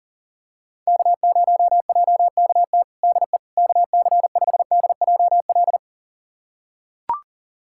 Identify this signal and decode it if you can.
Morse code, heard as a keyed tone, about 40 wpm, 700 Hz. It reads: K0JKT DE KC5DJL
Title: USA Callsign DE Callsign 40wpm